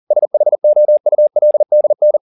Morse code, heard as a keyed tone, about 40 wpm, 600 Hz. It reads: SHOULDN